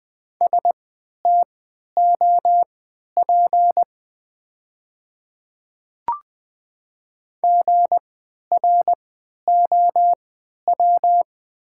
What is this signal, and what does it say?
Morse code, heard as a keyed tone, about 20 wpm, spaced as for 11 wpm, 700 Hz. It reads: STOP GROW
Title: 3x Character Spacing - Top 300 Words - 20wpm